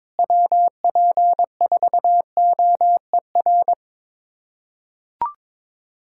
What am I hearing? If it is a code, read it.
Morse code, heard as a keyed tone, about 22 wpm, 700 Hz. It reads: WP4OER
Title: USA Callsign DE Callsign - 6 Characters - 22wpm